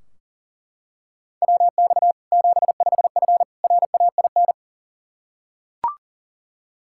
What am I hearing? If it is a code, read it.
Morse code, heard as a keyed tone, about 40 wpm, 700 Hz. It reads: WX 75F RAIN
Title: Practice QSO - Exchange 2 - Info - 40wpm